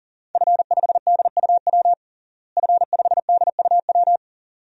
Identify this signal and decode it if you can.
Morse code, heard as a keyed tone, about 40 wpm, 700 Hz. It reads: F5DUW F5DUW
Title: DXCC-100 Call Signs - 5 Characters - Repeated 2x - 40wpm